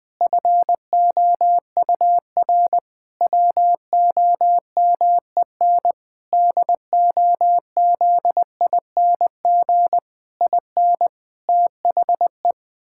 Morse code, heard as a keyed tone, about 21 wpm, 700 Hz. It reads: FOUR WOMEN DOZING IN THE
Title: Pangram Challenge v2 - 20wpm